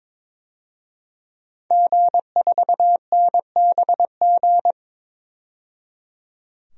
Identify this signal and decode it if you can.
Morse code, heard as a keyed tone, about 22 wpm, 700 Hz. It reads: G4NBG
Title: DXCC-100 Call Signs - 5 Characters - 22wpm